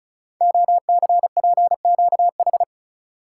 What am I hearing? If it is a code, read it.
Morse code, heard as a keyed tone, about 36 wpm, 700 Hz. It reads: OCPQH
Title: Code Groups - 5 Letter:Numbers - 35wpm